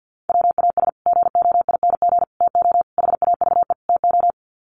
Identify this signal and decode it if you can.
Morse code, heard as a keyed tone, about 50 wpm, 700 Hz. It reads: WAS GOING TO HAVE TO
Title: Sets of 5 Words Encore 50wpm